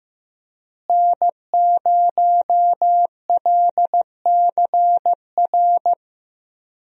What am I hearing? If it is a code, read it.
Morse code, heard as a keyed tone, about 15 wpm, 700 Hz. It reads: N0LCR
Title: USA Call Signs - 5 Characters - Repeated 2x - 15wpm